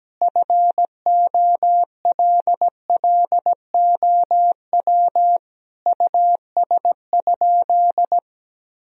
Morse code, heard as a keyed tone, about 17 wpm, 700 Hz. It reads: FOLLOW US?
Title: Sentences from Top 1000 Words - Review Part 1 - 17wpm